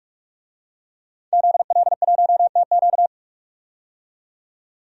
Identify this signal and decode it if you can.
Morse code, heard as a keyed tone, about 42 wpm, 700 Hz. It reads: ZL1TQ
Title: DXCC-100 Call Signs - 5 Characters - 45wpm